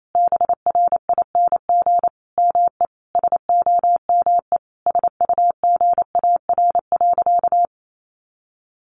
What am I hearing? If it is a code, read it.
Morse code, heard as a keyed tone, about 28 wpm, 700 Hz. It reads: BRING ME SOME SUGAR.
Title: Sentences from Top 1200 Words 28wpm